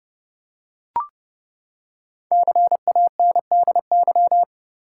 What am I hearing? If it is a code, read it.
Morse code, heard as a keyed tone, about 30 wpm, 700 Hz. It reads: CANDY
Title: Five Letter Words - 30wpm